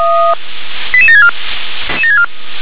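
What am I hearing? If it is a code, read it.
Five-tone selective call (ZVEI), heard as quick runs of five tones, tones of about 70 ms. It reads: 88764
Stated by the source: ZVEI five-tone selective call 88775, received with an SDR.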